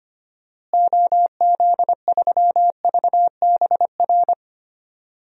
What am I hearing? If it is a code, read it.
Morse code, heard as a keyed tone, about 25 wpm, 700 Hz. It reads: OZ3VBR
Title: DXCC-100 Call Signs - 6 Characters - Repeated 2x - 25wpm